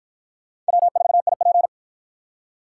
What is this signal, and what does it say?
Morse code, heard as a keyed tone, about 48 wpm, 700 Hz. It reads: W4IP